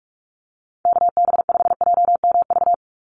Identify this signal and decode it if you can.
Morse code, heard as a keyed tone, about 44 wpm, 700 Hz. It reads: KB5JMV
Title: USA Call Signs 45wpm